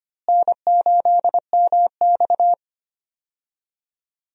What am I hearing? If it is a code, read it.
Morse code, heard as a keyed tone, about 25 wpm, 700 Hz. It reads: N8MX
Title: USA Call Signs - 4 Characters - Repeated 2x - 25wpm